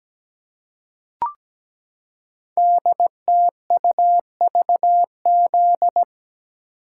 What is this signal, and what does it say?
Morse code, heard as a keyed tone, about 17 wpm, 700 Hz. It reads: DTUVZ
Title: Code Groups - 5 Letters - 17wpm